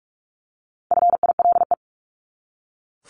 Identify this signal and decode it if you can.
Morse code, heard as a keyed tone, about 44 wpm, 700 Hz. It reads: FILE
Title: Top 1001-1250 Words 45wpm